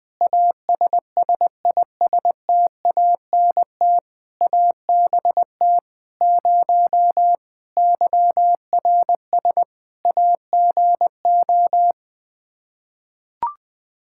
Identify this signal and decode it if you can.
Morse code, heard as a keyed tone, about 20 wpm, 700 Hz. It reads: ASSISTANT ABT 0 YRS AGO
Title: K1USN SST QSO Practice 20wpm